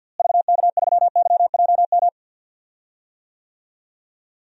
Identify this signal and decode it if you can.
Morse code, heard as a keyed tone, about 52 wpm, 700 Hz. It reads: VK3YJM
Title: DXCC-100 Call Signs - 6 Characters - Repeated 2x - 50wpm